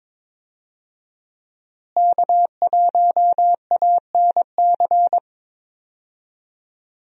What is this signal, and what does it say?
Morse code, heard as a keyed tone, about 22 wpm, 700 Hz. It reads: K1ANC
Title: USA Call Signs - 5 Characters - Repeated 2x - 22wpm